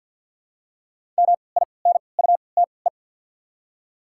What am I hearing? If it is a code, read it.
Morse code, heard as a keyed tone, about 53 wpm, spaced as for 29 wpm, 690 Hz. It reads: MINUTE
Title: 3x Character Spacing - Top 500 Words - 50wpm